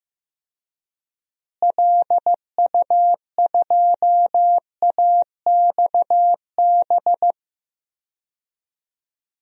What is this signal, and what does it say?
Morse code, heard as a keyed tone, about 15 wpm, 700 Hz. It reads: LU2AXB
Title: DXCC-100 Call Signs - 6 Characters - Repeated 3x - 15wpm